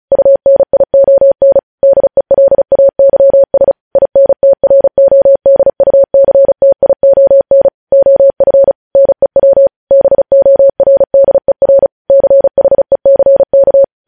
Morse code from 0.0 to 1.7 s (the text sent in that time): UNION